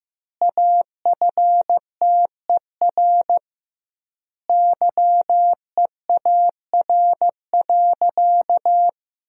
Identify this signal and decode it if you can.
Morse code, heard as a keyed tone, about 15 wpm, 700 Hz. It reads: AFTER YEAR.
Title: 2x Word-Spacing - Sentences from Top 300 Words - 15wpm